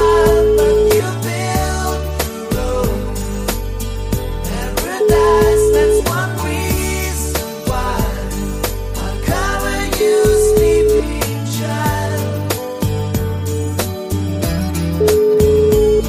Ringback tone (Europe, 425 Hz), with noise.